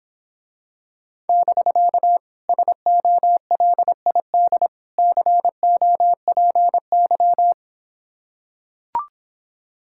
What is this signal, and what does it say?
Morse code, heard as a keyed tone, about 26 wpm, 700 Hz. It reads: <BK> SOLID COPY